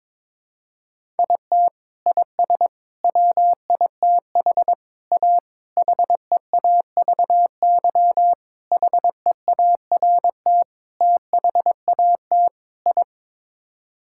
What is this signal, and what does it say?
Morse code, heard as a keyed tone, about 22 wpm, 700 Hz. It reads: IT IS WITH A HEAVY HEART THAT I
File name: practice / 22wpm / Sets of 8 Words - 22wpm